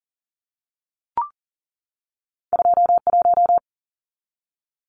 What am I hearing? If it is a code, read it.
Morse code, heard as a keyed tone, about 40 wpm, 700 Hz. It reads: 21
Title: Numbers - 2-Digit 40wpm